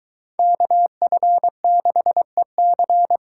Morse code, heard as a keyed tone, about 24 wpm, 700 Hz. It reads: KF6EC